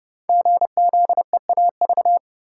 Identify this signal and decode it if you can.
Morse code, heard as a keyed tone, about 28 wpm, 700 Hz. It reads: GZEAV
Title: Code Groups - 5 Letter:Numbers - 30wpm